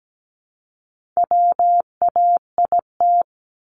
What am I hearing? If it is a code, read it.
Morse code, heard as a keyed tone, about 17 wpm, 700 Hz. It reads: WAIT
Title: ING Suffix - Top 100 Words with Base Words 17wpm